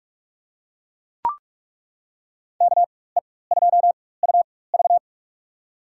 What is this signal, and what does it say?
Morse code, heard as a keyed tone, about 41 wpm, spaced as for 20 wpm, 700 Hz. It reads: KE2UV
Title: USA Call Signs - 5 Characters - 4x Character Spacing - 45wpm